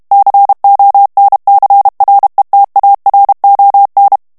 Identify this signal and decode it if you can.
Morse code, heard as a keyed tone, about 32 wpm, 800 Hz. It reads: CONCRETARON
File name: morse mp3 / ESPAGNOL TEXTE EN CLAIR 1 VITESSE 30 MOTS MINUTES